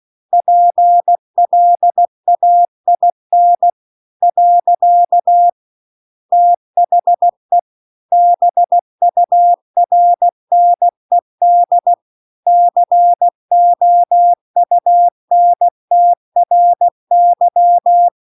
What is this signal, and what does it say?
Morse code, heard as a keyed tone, about 16 wpm, 700 Hz. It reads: PLAIN . THE BURNED COUNTRY